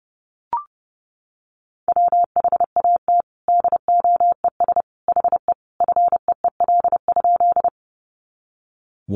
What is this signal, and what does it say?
Morse code, heard as a keyed tone, about 30 wpm, 700 Hz. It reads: WHAT DOES HE FEEL?
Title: Sentences from Top 400 Words 30wpm